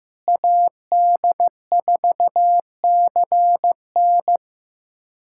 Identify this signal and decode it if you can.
Morse code, heard as a keyed tone, about 15 wpm, 700 Hz. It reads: AD4CN
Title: USA Callsign DE Callsign - 5 Characters - 15wpm